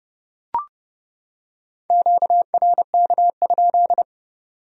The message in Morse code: QRK?

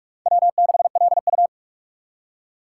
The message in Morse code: W6LU

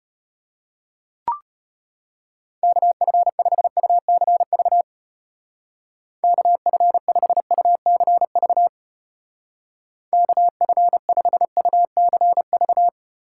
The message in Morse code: KF5UCV KF5UCV KF5UCV

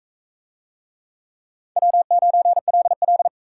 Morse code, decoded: W9LL

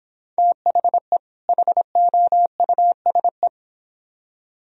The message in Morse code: THE HOUSE

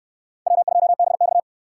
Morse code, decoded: W2BX

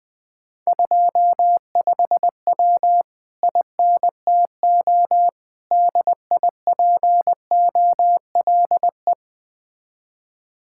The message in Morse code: 25W INTO DIPOLE

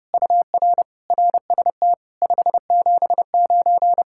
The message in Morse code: UR RST 579

